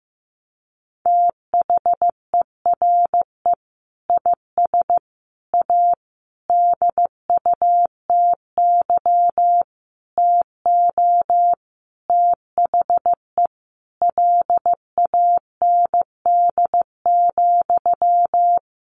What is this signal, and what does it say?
Morse code, heard as a keyed tone, about 15 wpm, 700 Hz. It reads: THERE IS A DUTY TO THE LAND,
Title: Sentences from Top 1500 Words 15wpm